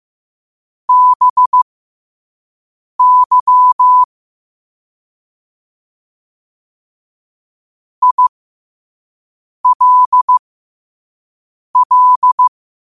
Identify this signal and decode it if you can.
Morse code, heard as a keyed tone, about 15 wpm, spaced as for 5 wpm, 1000 Hz. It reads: BY ILL
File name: qotdfiles / 5wpm